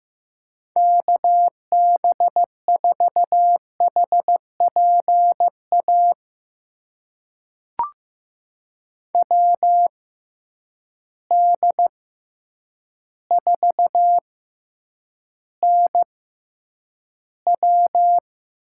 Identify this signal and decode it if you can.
Morse code, heard as a keyed tone, about 15 wpm, spaced as for 8 wpm, 700 Hz. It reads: KB4HPA WD4NW
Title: USA Call Signs - 6 Characters - 6x Character Spacing - 15wpm